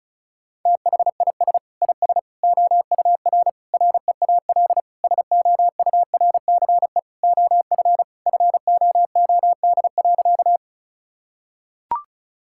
Morse code, heard as a keyed tone, about 36 wpm, 700 Hz. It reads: THIS IS OUR REAL SOURCE OF FOOD.
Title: Sentences from Top 1000 Words - Review Part 5 - 35wpm